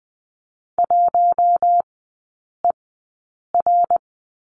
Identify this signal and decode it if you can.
Morse code, heard as a keyed tone, about 20 wpm, spaced as for 8 wpm, 700 Hz. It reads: 1ER
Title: 074 Lesson 13e - Letter L - Three Characters 20wpm